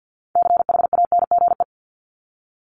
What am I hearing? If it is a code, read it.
Morse code, heard as a keyed tone, about 50 wpm, 700 Hz. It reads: CHANGE
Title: ES:S Suffix - Top 100 Words with Base Words 50wpm